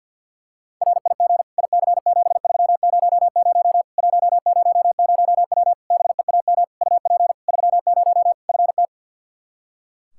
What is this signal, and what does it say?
Morse code, heard as a keyed tone, about 48 wpm, 700 Hz. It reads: RIG IC7300 100W BEAM UP 30 FT
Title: Practice QSO - Exchange 2 - Info - 50wpm